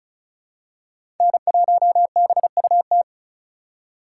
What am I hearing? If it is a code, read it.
Morse code, heard as a keyed tone, about 34 wpm, 700 Hz. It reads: N1BUT
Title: USA Call Signs - 5 Characters - 4x Character Spacing - 35wpm